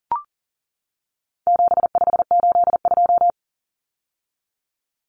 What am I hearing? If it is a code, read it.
Morse code, heard as a keyed tone, about 40 wpm, 700 Hz. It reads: 7582
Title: Numbers - 4-Digit 40wpm